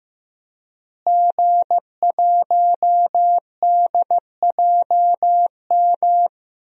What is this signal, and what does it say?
Morse code, heard as a keyed tone, about 15 wpm, 700 Hz. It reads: G1DJM
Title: DXCC-100 Call Signs - 5 Characters - Repeated 2x - 15wpm